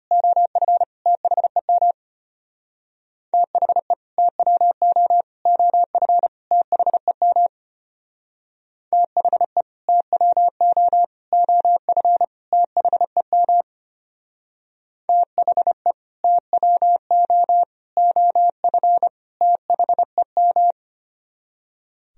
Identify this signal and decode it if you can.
Morse code, heard as a keyed tone, about 32 wpm, 700 Hz. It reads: OF THEM THE TWO OF THEM THE TWO OF THEM THE TWO OF THEM